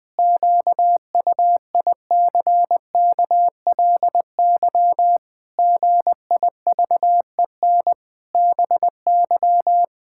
Morse code, heard as a keyed tone, about 20 wpm, 700 Hz. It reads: QUICKLY GIVEN BY
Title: Pangram Challenge v2 - 20wpm